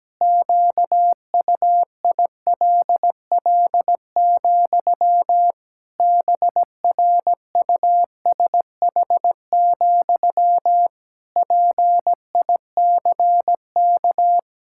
Morse code, heard as a keyed tone, about 17 wpm, 700 Hz. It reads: QUILL, BRUSH, PICK
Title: Pangram Challenge v2 - 17wpm